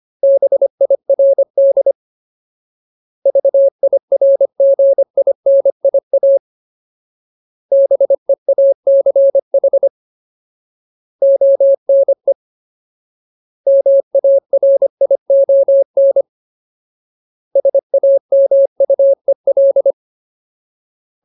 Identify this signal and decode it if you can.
Morse code, heard as a keyed tone, about 25 wpm, 550 Hz. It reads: BIRD VIRGINIA BEACH ONE MARION SAMUEL
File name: Ultimate Word List / 25WPM Ultimate